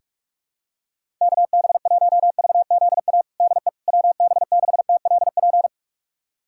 Morse code, heard as a keyed tone, about 46 wpm, 700 Hz. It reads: KB1VZA DE WB6TLP
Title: USA Callsign DE Callsign - 6 Characters - 45wpm